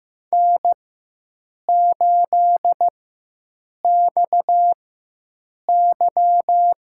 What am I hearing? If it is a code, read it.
Morse code, heard as a keyed tone, about 15 wpm, spaced as for 7 wpm, 700 Hz. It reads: N8XY
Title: USA Call Signs - 4x Character Spacing - 15wpm